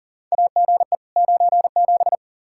Morse code, heard as a keyed tone, about 37 wpm, 700 Hz. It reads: AGE 97
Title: Practice QSO - Exchange 3 - More Info - 40wpm